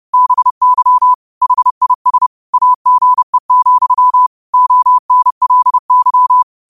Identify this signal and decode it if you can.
Morse code, heard as a keyed tone, about 30 wpm, 1000 Hz. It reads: BY HIS AGE, ONLY
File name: qotdfiles / 30wpm